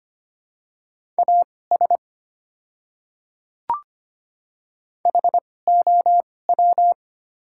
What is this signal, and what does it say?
Morse code, heard as a keyed tone, about 25 wpm, spaced as for 18 wpm, 690 Hz. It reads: AS HOW
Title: 2x Character Spacing - Top 100 Words - 25wpm